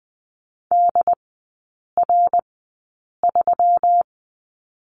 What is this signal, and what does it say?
Morse code, heard as a keyed tone, about 20 wpm, spaced as for 8 wpm, 700 Hz. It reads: DR3
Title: 121 Lesson 20e - Number 3 - Three Characters 20wpm